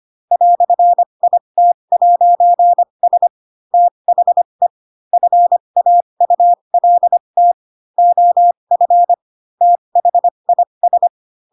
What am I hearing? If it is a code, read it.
Morse code, heard as a keyed tone, about 25 wpm, spaced as for 22 wpm, 700 Hz. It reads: "IT'S THE FAULT OF THIS
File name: Pinocchio / Pinocchio ch2 18 25 ik1hge